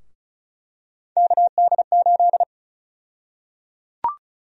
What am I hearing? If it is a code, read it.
Morse code, heard as a keyed tone, about 36 wpm, 700 Hz. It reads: KD8
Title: USA Call Sign Prefixes - Repeated 2x - 35wpm